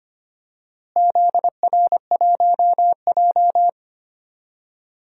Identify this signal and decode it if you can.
Morse code, heard as a keyed tone, about 25 wpm, 700 Hz. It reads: ZR1J
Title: DXCC-100 Call Signs - 4 Characters - Repeated 2x - 25wpm